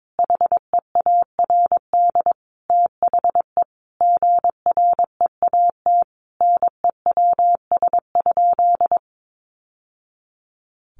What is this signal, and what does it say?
Morse code, heard as a keyed tone, about 22 wpm, 700 Hz. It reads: HEARD THE GREAT NEWS?
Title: Sentences from Top 800 Words 22wpm